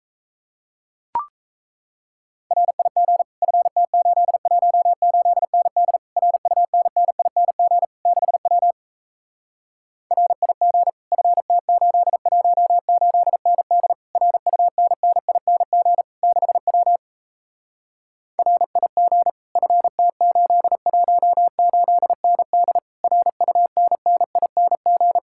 Morse code, heard as a keyed tone, about 37 wpm, 700 Hz. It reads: RIG FT818ND RUNNING 6W RIG FT818ND RUNNING 6W RIG FT818ND RUNNING